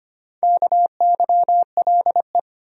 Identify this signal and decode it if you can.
Morse code, heard as a keyed tone, about 25 wpm, 700 Hz. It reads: KYLE
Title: USA Common Names - Max 5 Letters - 25wpm